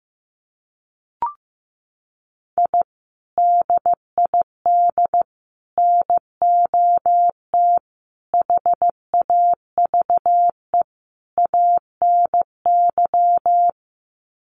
Morse code, heard as a keyed tone, about 15 wpm, 700 Hz. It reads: I DID NOT HAVE ANY
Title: Sets of 5 Words Encore 15wpm